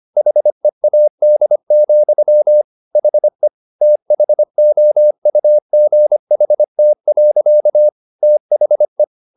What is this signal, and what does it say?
Morse code, heard as a keyed tone, about 25 wpm, 600 Hz. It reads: HEAD, HE THOUGHT. THE